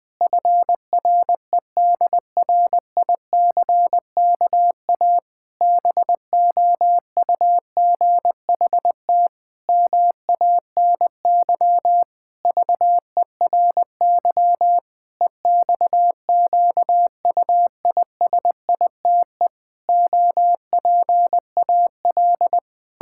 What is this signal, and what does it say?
Morse code, heard as a keyed tone, about 20 wpm, 700 Hz. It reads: FREDRICKA BOUGHT MANY VERY EXQUISITE OPAL
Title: Pangram Challenge v2 - 20wpm